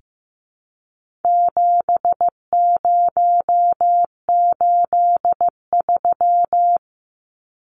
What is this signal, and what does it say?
Morse code, heard as a keyed tone, about 15 wpm, 700 Hz. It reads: 7083
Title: Numbers - 4-Digit 15wpm